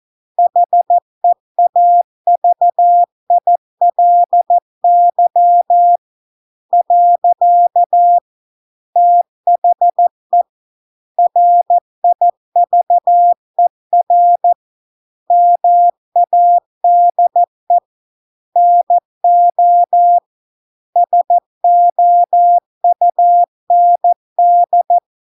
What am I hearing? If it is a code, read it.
Morse code, heard as a keyed tone, about 14 wpm, 700 Hz. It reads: HEAVILY . THE RIVER MADE NO SOUND